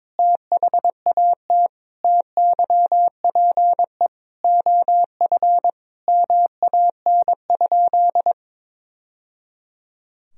Morse code, heard as a keyed tone, about 22 wpm, 700 Hz. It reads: THAT TYPE OF MAN?
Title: Sentences from Top 1000 Words - Review Part 3 - 22wpm